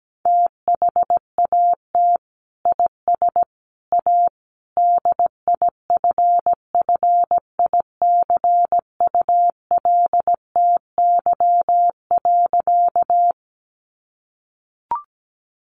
Morse code, heard as a keyed tone, about 17 wpm, 700 Hz. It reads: THAT IS A DIFFICULTY.
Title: Sentences from Top 1500 Words 17wpm